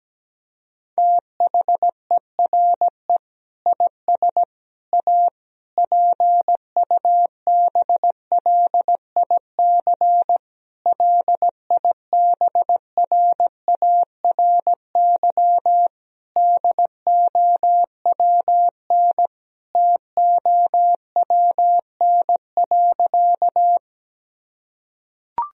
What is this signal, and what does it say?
Morse code, heard as a keyed tone, about 17 wpm, 700 Hz. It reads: THERE IS A PUBLIC LIBRARY DOWN TOWN.
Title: Sentences from Top 1000 Words - Review Part 4 - 17wpm